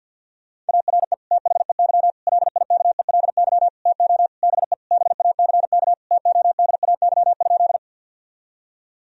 Morse code, heard as a keyed tone, about 50 wpm, 700 Hz. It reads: ARE THEY LIKELY TO BE BACK TODAY?
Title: Sentences from Top 1000 Words - Review Part 4 - 50wpm